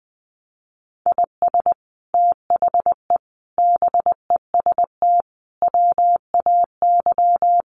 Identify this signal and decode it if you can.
Morse code, heard as a keyed tone, about 20 wpm, 700 Hz. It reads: IS THE BEST WAY
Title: Sets of 5 Words Encore 20wpm